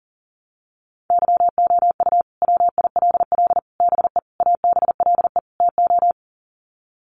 Morse code, heard as a keyed tone, about 37 wpm, 700 Hz. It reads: YOU WILL BE ABLE TO
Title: Sets of 5 Words Encore2 40wpm